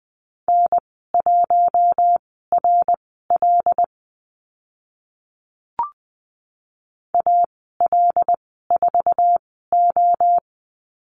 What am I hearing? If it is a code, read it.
Morse code, heard as a keyed tone, about 20 wpm, spaced as for 14 wpm, 700 Hz. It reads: N1RL AL4O
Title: 076 Lesson 13g - Letter L - Call Signs 20wpm